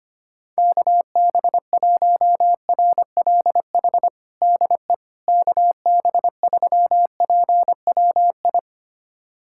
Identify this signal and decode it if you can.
Morse code, heard as a keyed tone, about 25 wpm, 700 Hz. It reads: KB1RLH DE KB3PWI
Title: USA Callsign DE Callsign 25wpm